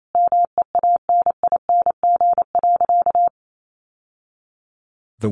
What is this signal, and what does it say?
Morse code, heard as a keyed tone, about 28 wpm, 700 Hz. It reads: MEANING.
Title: Sentences from Top 1100 Words 28wpm